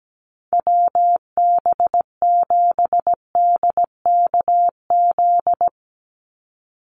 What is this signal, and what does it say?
Morse code, heard as a keyed tone, about 17 wpm, 700 Hz. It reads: WB7DKZ
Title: Call Signs - 17wpm